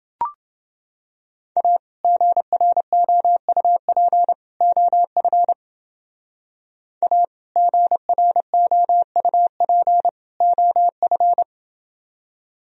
Morse code, heard as a keyed tone, about 30 wpm, 700 Hz. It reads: A GROUP OF A GROUP OF